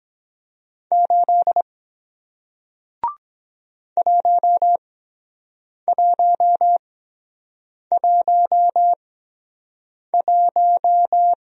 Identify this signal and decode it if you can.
Morse code, heard as a keyed tone, about 23 wpm, spaced as for 7 wpm, 700 Hz. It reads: 8 1111